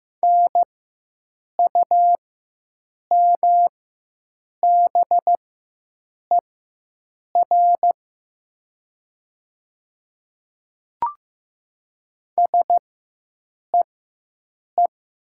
Morse code, heard as a keyed tone, about 15 wpm, spaced as for 7 wpm, 700 Hz. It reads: NUMBER SEE
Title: 4x Character Spacing - Top 100 Words - 15wpm